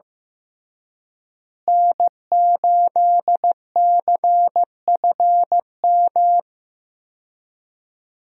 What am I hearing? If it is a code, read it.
Morse code, heard as a keyed tone, about 15 wpm, 700 Hz. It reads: N8CFM
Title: USA Call Signs - 5 Characters - Repeated 3x - 15wpm